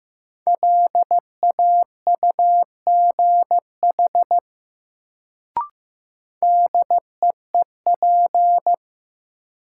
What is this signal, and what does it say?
Morse code, heard as a keyed tone, about 15 wpm, 700 Hz. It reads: LAUGH DEEP
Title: Sentences from Top 500 Words - New Word Drill - 15wpm